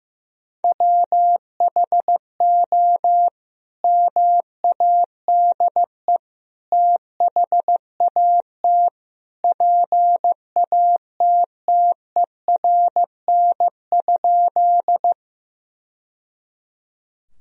Morse code, heard as a keyed tone, about 16 wpm, 700 Hz. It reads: WHO MADE THAT PATTERN?
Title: Sentences from Top 1000 Words - Review Part 2 - 15wpm